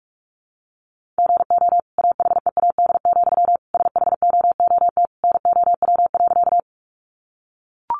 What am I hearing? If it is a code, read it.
Morse code, heard as a keyed tone, about 44 wpm, 700 Hz. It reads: GO AHEAD, SHOOT NOW.
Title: Sentences from Top 1200 Words 45wpm